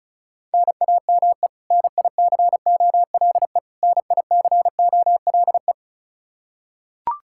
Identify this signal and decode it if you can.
Morse code, heard as a keyed tone, about 34 wpm, 700 Hz. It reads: NAME NICOLE NICOLE
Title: Practice QSO - Exchange 1 - Beginning - 35wpm